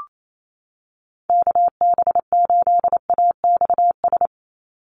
Morse code, heard as a keyed tone, about 28 wpm, 700 Hz. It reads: KB8AXS
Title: USA Call Signs 28wpm